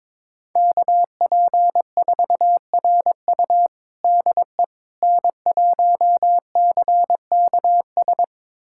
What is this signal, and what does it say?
Morse code, heard as a keyed tone, about 22 wpm, 700 Hz. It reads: KP4RU DE N1CKS
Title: USA Callsign DE Callsign - 5 Characters - 22wpm